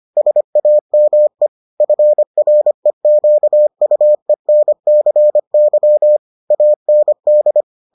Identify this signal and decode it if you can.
Morse code, heard as a keyed tone, about 25 wpm, 600 Hz. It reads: SAME FREQUENCY AND